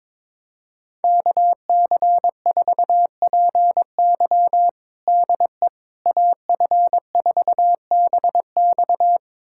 Morse code, heard as a keyed tone, about 22 wpm, 700 Hz. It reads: KC4PY DE AF4BX
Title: USA Callsign DE Callsign - 5 Characters - 22wpm